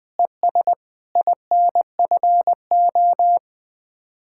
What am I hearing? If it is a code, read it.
Morse code, heard as a keyed tone, about 20 wpm, 700 Hz. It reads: ES INFO